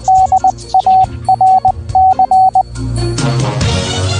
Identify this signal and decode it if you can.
Morse code, heard as a keyed tone, about 20 wpm, 760 Hz. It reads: DARC
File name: radio-darc / DARC-Radio- -64K